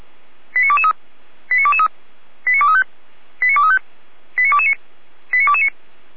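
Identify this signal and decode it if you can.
Five-tone selective call (ZVEI), heard as quick runs of five tones, tones of about 70 ms. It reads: 89223 89223 89235 89235 89229 89229
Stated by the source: ZVEI five-tone selective call received on SDR.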